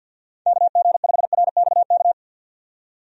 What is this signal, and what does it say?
Morse code, heard as a keyed tone, about 48 wpm, 700 Hz. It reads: KG5RXK
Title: USA Call Signs - 6 Characters - 50wpm